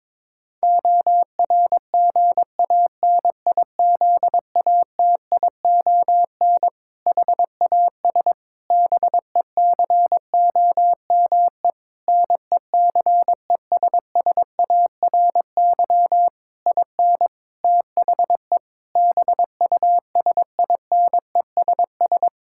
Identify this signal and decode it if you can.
Morse code, heard as a keyed tone, about 21 wpm, 700 Hz. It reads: ORGANIZATION HAS BECOME NECESSARY IN THE BUSINESS
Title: Sentences from Top 1000 Words - Review Part 5 - 22wpm